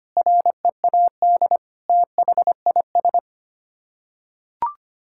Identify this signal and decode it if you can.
Morse code, heard as a keyed tone, about 25 wpm, 700 Hz. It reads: READ THIS
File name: Sets of 9 Words - 25wpm